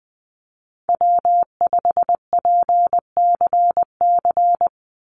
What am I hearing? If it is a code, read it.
Morse code, heard as a keyed tone, about 20 wpm, 700 Hz. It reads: W5PCC
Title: USA Call Signs 20wpm